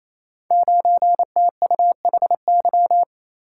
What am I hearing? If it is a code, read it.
Morse code, heard as a keyed tone, about 27 wpm, 700 Hz. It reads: 9TUHY
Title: Code Groups - 5 Letter:Numbers - 28wpm